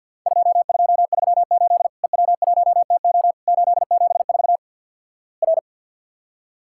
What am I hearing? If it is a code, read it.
Morse code, heard as a keyed tone, about 50 wpm, 700 Hz. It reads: 2238 EW1TO 874 R
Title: Worked All Europe DX Contest - QTC Exchange - 50wpm